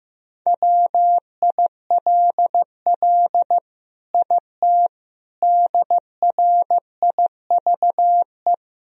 Morse code, heard as a keyed tone, about 15 wpm, 700 Hz. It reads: WILL IT DRIVE
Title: Sentences from Top 1000 Words - Review Part 2 - 15wpm